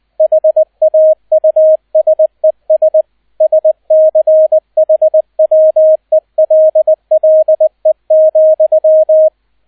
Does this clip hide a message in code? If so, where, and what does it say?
Yes: Morse code from 0.2 to 9.3 s, heard as a keyed tone, about 20 wpm, 630 Hz, with radio noise. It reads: HAUSES SCHWELLE,